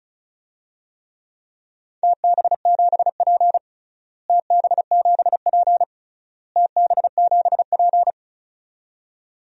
Morse code, heard as a keyed tone, about 34 wpm, 700 Hz. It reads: TB7P TB7P TB7P
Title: DXCC-100 Call Signs - 4 Characters - Repeated 3x - 35wpm